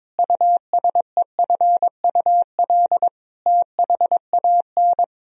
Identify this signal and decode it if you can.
Morse code, heard as a keyed tone, about 22 wpm, 700 Hz. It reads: USEFUL THAN
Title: Sentences from Top 1000 Words - Review Part 4 - 22wpm